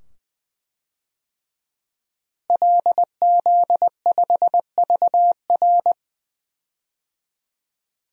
Morse code, heard as a keyed tone, about 19 wpm, 700 Hz. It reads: LZ5VR
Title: DXCC-100 Call Signs - 5 Characters - Repeated 2x - 20wpm